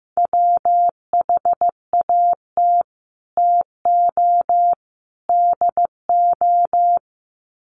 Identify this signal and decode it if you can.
Morse code, heard as a keyed tone, about 15 wpm, 700 Hz. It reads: WHAT TO DO
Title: Sets of 3 Words 15wpm